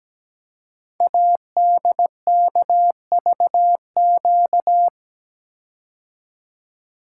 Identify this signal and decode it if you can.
Morse code, heard as a keyed tone, about 17 wpm, 700 Hz. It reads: ADKVQ
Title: Code Groups - 5 Letters - 17wpm